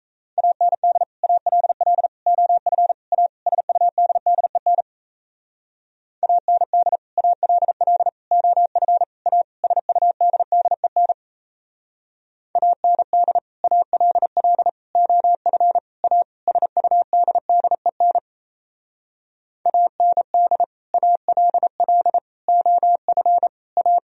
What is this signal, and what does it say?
Morse code, heard as a keyed tone, about 35 wpm, 700 Hz. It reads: AND ALL OF A SUDDEN AND ALL OF A SUDDEN AND ALL OF A SUDDEN AND ALL OF A
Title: Speed-Racing - Sets of 5 Words 28wpm